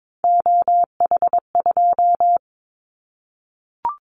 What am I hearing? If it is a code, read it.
Morse code, heard as a keyed tone, about 22 wpm, 700 Hz. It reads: OH2
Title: Callsign Prefixes for Top 50 DXCC - 22wpm